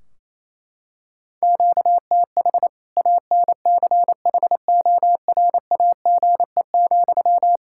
Morse code, heard as a keyed tone, about 28 wpm, 700 Hz. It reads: QTH ANCHORAGE,
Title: K1USN SST QSO Practice 28wpm